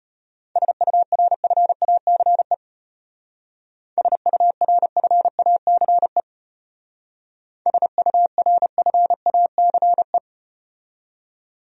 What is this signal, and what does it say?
Morse code, heard as a keyed tone, about 32 wpm, 700 Hz. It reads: SURFACE SURFACE SURFACE